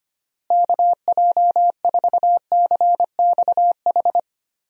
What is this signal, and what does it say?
Morse code, heard as a keyed tone, about 25 wpm, 700 Hz. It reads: KJ4CXH
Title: USA Call Signs - 6 Characters - Repeated 3x - 25wpm